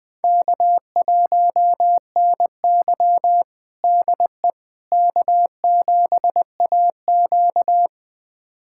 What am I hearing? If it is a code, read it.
Morse code, heard as a keyed tone, about 20 wpm, 700 Hz. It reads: K1NY DE K7AQ
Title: USA Callsign DE Callsign - 4 Characters - 20wpm